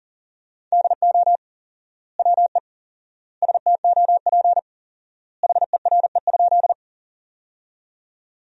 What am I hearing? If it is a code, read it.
Morse code, heard as a keyed tone, about 38 wpm, 700 Hz. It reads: DO WE STOP HERE?
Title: 4x Word-Spacing - Sentences from Top 300 Words - 40wpm